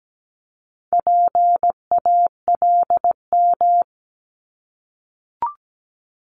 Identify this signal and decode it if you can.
Morse code, heard as a keyed tone, about 17 wpm, 700 Hz. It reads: PALM
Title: Multi-letter Phonemes Repeated 3x 17wpm